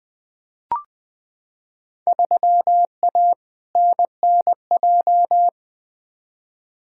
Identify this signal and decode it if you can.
Morse code, heard as a keyed tone, about 20 wpm, 700 Hz. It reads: 3A NNJ
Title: Field Day Exchanges - 20wpm